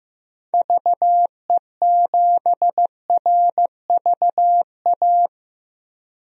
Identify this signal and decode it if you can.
Morse code, heard as a keyed tone, about 15 wpm, 700 Hz. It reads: VE7RVA
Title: DXCC-100 Call Signs - 6 Characters - 15wpm